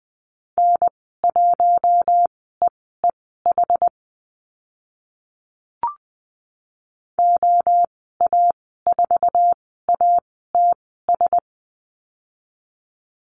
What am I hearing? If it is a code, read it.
Morse code, heard as a keyed tone, about 20 wpm, spaced as for 15 wpm, 700 Hz. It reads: N1EEH OA4ATS
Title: 082 Lesson 14g - Number 2 - Call Signs 20wpm